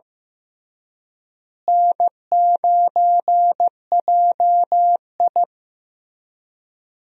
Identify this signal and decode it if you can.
Morse code, heard as a keyed tone, about 15 wpm, 700 Hz. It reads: N9JI
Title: USA Call Signs - 4 Characters - Repeated 3x - 15wpm